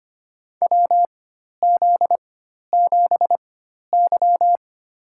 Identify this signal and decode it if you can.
Morse code, heard as a keyed tone, about 25 wpm, spaced as for 12 wpm, 700 Hz. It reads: WZ7Y
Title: USA Call Signs - 4x Character Spacing - 25wpm